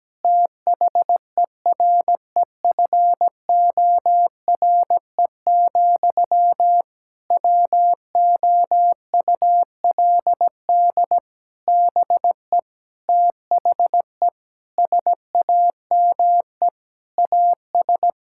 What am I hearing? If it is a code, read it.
Morse code, heard as a keyed tone, about 17 wpm, 700 Hz. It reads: THEREFORE, WOULD BE THE SAME AS
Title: Sentences from Top 1000 Words - Review Part 5 - 17wpm